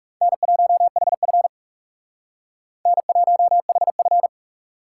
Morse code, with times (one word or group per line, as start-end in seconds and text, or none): 0.2-1.5 N1HF
2.9-4.3 N1HF